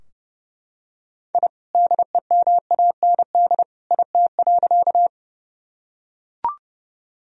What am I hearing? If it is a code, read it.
Morse code, heard as a keyed tone, about 32 wpm, 700 Hz. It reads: I DEMAND IT.
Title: Sentences from Top 1000 Words - Review Part 4 - 30wpm